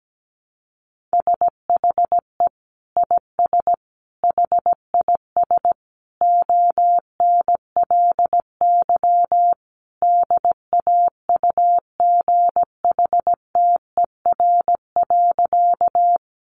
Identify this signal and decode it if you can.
Morse code, heard as a keyed tone, about 17 wpm, 700 Hz. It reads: SHE IS HIS ONLY DAUGHTER.
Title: Sentences from Top 1500 Words 17wpm